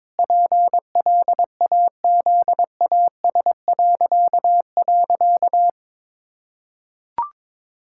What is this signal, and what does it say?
Morse code, heard as a keyed tone, about 22 wpm, 700 Hz. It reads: PLAZAS..
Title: Pangram Challenge v2 - 22wpm